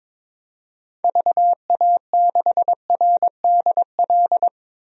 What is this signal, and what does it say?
Morse code, heard as a keyed tone, about 22 wpm, 700 Hz. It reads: VA6RDL
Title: DXCC-100 Call Signs - 6 Characters - Repeated 3x - 22wpm